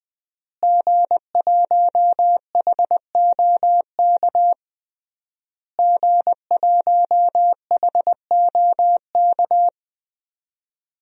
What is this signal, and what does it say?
Morse code, heard as a keyed tone, about 21 wpm, 700 Hz. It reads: G1HOK G1HOK
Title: DXCC-100 Call Signs - 5 Characters - Repeated 3x - 20wpm